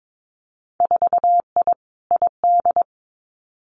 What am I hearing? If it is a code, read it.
Morse code, heard as a keyed tone, about 22 wpm, 700 Hz. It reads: 4I ID
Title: Winter Field Day Exchanges 22wpm